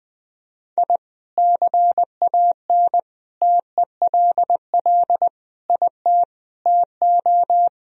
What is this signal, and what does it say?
Morse code, heard as a keyed tone, about 20 wpm, 700 Hz. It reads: I CAN TELL IT TO
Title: Sentences from Top 1000 Words - Review Part 1 - 20wpm